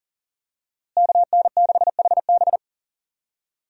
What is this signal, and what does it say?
Morse code, heard as a keyed tone, about 40 wpm, 700 Hz. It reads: KN6HB
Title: USA Call Signs - 5 Characters - Repeated 3x - 40wpm